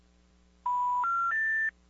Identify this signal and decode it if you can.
Special information tone.
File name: 575-385-9999 valor